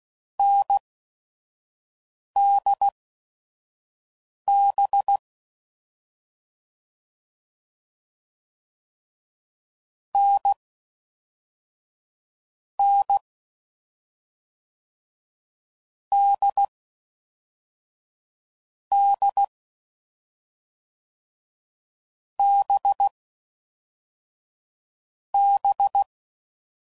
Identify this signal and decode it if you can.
Morse code, heard as a keyed tone, about 16 wpm, spaced as for 5 wpm, 800 Hz. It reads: NDB N N D D B B
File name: Combinations / Difficult Combos / awj1 ndb6 gz7